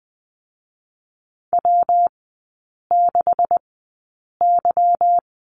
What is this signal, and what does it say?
Morse code, heard as a keyed tone, about 20 wpm, spaced as for 8 wpm, 700 Hz. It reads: W6Y
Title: 145 Lesson 24e - Letter y - Three Characters 20wpm